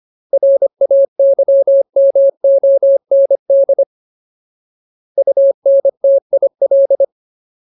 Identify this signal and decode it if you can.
Morse code, heard as a keyed tone, about 25 wpm, 550 Hz. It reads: RAYMOND UNTIL
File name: Ultimate Word List / 25WPM Ultimate